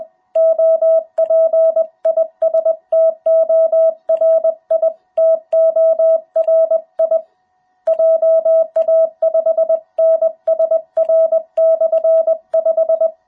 Morse code, heard as a keyed tone, about 20 wpm, 660 Hz. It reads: OP IS TORI TORI JA5NSR/5
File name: jh8cbh / 2021cw